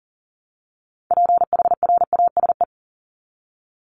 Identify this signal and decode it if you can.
Morse code, heard as a keyed tone, about 40 wpm, 700 Hz. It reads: PHRASE
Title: Top 1501-1750 Words 40wpm